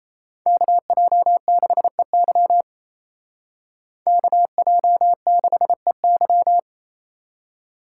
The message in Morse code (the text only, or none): KJ6EY KJ6EY